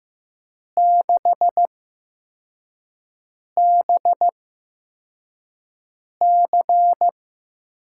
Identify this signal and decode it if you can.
Morse code, heard as a keyed tone, about 15 wpm, spaced as for 4 wpm, 700 Hz. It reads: 6BC
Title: USA Call Signs - 5 Characters - 8x Character Spacing - 15wpm